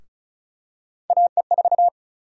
Morse code, heard as a keyed tone, about 34 wpm, 700 Hz. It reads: AE4
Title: USA Call Sign Prefixes - 35wpm